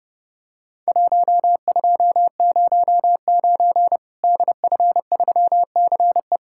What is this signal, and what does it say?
Morse code, heard as a keyed tone, about 31 wpm, 700 Hz. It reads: 1209 DF3CE
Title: Worked All Europe DX Contest - QTC Exchange - 30wpm